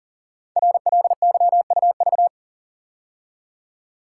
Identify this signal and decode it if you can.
Morse code, heard as a keyed tone, about 42 wpm, 690 Hz. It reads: RLYUV
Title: Code Groups - 5 Letters - 40wpm